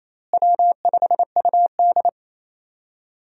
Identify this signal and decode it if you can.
Morse code, heard as a keyed tone, about 27 wpm, 700 Hz. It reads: W5UD
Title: USA Call Signs - 4 Characters - Repeated 3x - 28wpm